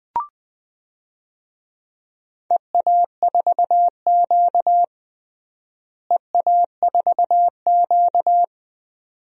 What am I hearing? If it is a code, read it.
Morse code, heard as a keyed tone, about 19 wpm, 700 Hz. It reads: EA4Q EA4Q
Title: DXCC-100 Call Signs - 4 Characters - Repeated 2x - 20wpm